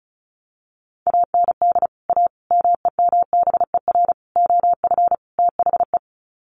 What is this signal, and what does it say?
Morse code, heard as a keyed tone, about 34 wpm, 700 Hz. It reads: AND A MEMBER OF THE
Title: Sets of 5 Words 35wpm